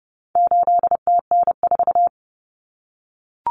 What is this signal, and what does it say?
Morse code, heard as a keyed tone, about 30 wpm, 700 Hz. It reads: 8TN4
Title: Common Cut Numbers - 4-Digit 30wpm